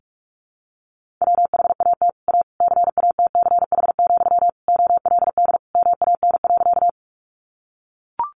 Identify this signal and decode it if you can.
Morse code, heard as a keyed tone, about 44 wpm, 700 Hz. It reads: WHAT A CATCH, OLD MAN.
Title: Sentences from Top 1500 Words 45wpm